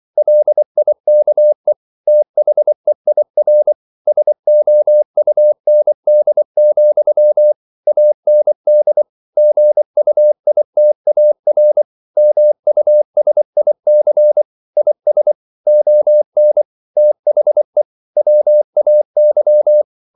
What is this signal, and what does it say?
Morse code, heard as a keyed tone, about 24 wpm, 600 Hz. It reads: LIKE THEIR SOUND, AND GUITAR MUSIC IS ON THE WAY
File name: CW / Practice / EnglishQuotations0424